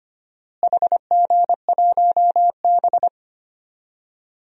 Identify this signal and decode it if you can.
Morse code, heard as a keyed tone, about 25 wpm, 700 Hz. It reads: HG1B
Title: DXCC-100 Call Signs - 4 Characters - Repeated 3x - 25wpm